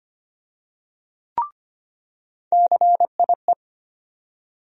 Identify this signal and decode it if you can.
Morse code, heard as a keyed tone, about 25 wpm, 700 Hz. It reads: CIE
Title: USA Call Sign Postfixes - 3 Letters 25wpm